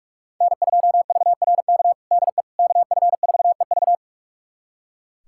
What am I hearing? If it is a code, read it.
Morse code, heard as a keyed tone, about 46 wpm, 700 Hz. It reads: N2VRK DE KF4EV
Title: USA Callsign DE Callsign - 5 Characters - 45wpm